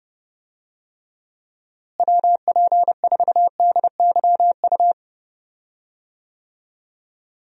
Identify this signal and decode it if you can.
Morse code, heard as a keyed tone, about 32 wpm, 700 Hz. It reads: WP4DYU
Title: USA Call Signs - 6 Characters - Repeated 2x - 30wpm